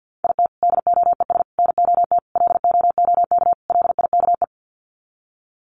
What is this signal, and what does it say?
Morse code, heard as a keyed tone, about 50 wpm, 700 Hz. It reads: IT DOES NOT LOOK LIKE
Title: Sets of 5 Words Encore2 50wpm